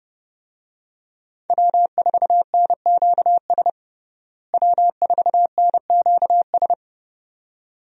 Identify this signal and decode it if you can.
Morse code, heard as a keyed tone, about 30 wpm, 700 Hz. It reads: W4NQS W4NQS
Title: USA Call Signs - 5 Characters - Repeated 2x - 30wpm